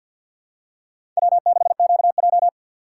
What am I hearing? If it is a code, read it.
Morse code, heard as a keyed tone, about 51 wpm, 700 Hz. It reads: W6QJ
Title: USA Call Signs - 4 Characters - 3x Character Spacing - 50wpm